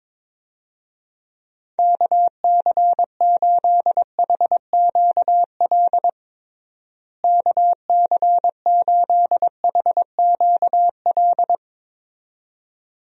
Morse code, heard as a keyed tone, about 22 wpm, 700 Hz. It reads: KC8HQL KC8HQL
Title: USA Call Signs - 6 Characters - Repeated 2x - 22wpm